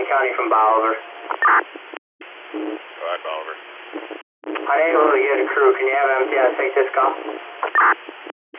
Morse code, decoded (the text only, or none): none